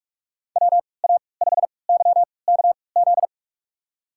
Morse code, heard as a keyed tone, about 42 wpm, spaced as for 25 wpm, 700 Hz. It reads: WA5YKZ
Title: USA Call Signs - 6 Characters - 3x Character Spacing - 45wpm